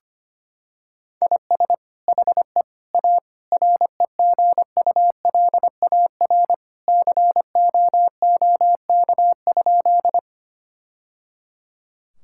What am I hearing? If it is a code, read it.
Morse code, heard as a keyed tone, about 25 wpm, 700 Hz. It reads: IS HE A REGULAR COOK?
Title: Sentences from Top 1000 Words - Review Part 6 - 25wpm